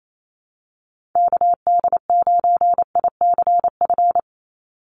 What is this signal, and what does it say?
Morse code, heard as a keyed tone, about 28 wpm, 700 Hz. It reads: KD9ICF
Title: USA Call Signs 28wpm